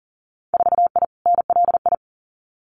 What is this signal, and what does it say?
Morse code, heard as a keyed tone, about 37 wpm, 700 Hz. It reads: 4I NLI
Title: Winter Field Day Exchanges Encore 40wpm